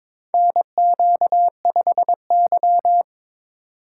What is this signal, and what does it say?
Morse code, heard as a keyed tone, about 23 wpm, 700 Hz. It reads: NQ5Y
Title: USA Call Signs - 4 Characters - 4x Character Spacing - 22wpm